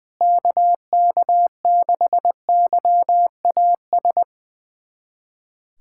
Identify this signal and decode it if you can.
Morse code, heard as a keyed tone, about 21 wpm, 700 Hz. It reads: KK6YAS
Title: USA Callsign DE Callsign - 6 Characters - 20wpm